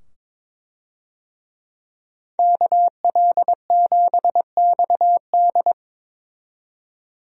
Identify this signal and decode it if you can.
Morse code, heard as a keyed tone, about 22 wpm, 700 Hz. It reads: KL7XD
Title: DXCC-100 Call Signs - 5 Characters - 22wpm